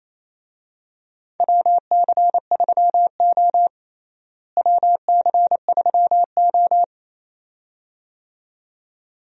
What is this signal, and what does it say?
Morse code, heard as a keyed tone, about 28 wpm, 700 Hz. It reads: WC3O WC3O
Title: USA Call Signs - 4 Characters - Repeated 2x - 28wpm